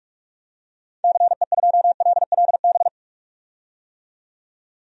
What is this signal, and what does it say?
Morse code, heard as a keyed tone, about 43 wpm, 700 Hz. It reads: CE2LLB
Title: DXCC-100 Call Signs - 6 Characters - 45wpm